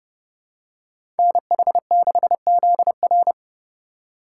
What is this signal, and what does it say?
Morse code, heard as a keyed tone, about 30 wpm, 700 Hz. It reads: NH6ZR
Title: USA Call Signs - 5 Characters - 4x Character Spacing - 30wpm